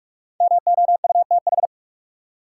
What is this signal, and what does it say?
Morse code, heard as a keyed tone, about 45 wpm, 700 Hz. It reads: MOUTH
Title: Sentences from Top 1500 Words - New Word Drill - 45wpm